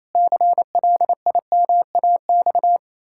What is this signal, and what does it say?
Morse code, heard as a keyed tone, about 28 wpm, 700 Hz. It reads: CLIMAX
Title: Pangram Challenge v2 - 28wpm